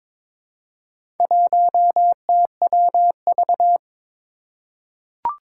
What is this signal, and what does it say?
Morse code, heard as a keyed tone, about 22 wpm, 700 Hz. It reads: 1TWV
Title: Four Challenging Letter-Number Combinations 22wpm